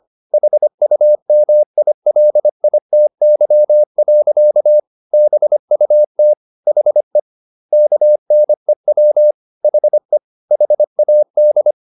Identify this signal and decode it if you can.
Morse code, heard as a keyed tone, about 25 wpm, 600 Hz. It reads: HUMILITY. BUT HE KNEW HE HAD
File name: the old man and the sea part 1 25wpm